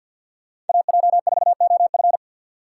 Morse code, heard as a keyed tone, about 52 wpm, 700 Hz. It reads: AJ4OF
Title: USA Call Signs - 5 Characters - 2x Character Spacing - 50wpm